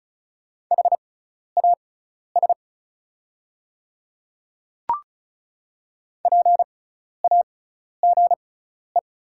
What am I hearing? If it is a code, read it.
Morse code, heard as a keyed tone, about 34 wpm, spaced as for 12 wpm, 690 Hz. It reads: HAS PAGE
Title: 6x Character Spacing - Top 300 Words - 35wpm